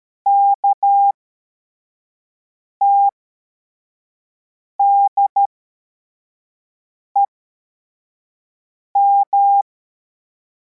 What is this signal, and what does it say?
Morse code, heard as a keyed tone, about 13 wpm, spaced as for 4 wpm, 800 Hz. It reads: KTDEM
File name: formation / cw / Text25